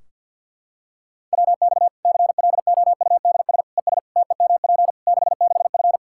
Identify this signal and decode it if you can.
Morse code, heard as a keyed tone, about 47 wpm, 700 Hz. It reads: WX CLOUDS ES TEMP 66F
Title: Practice QSO - Exchange 2 - Info - 50wpm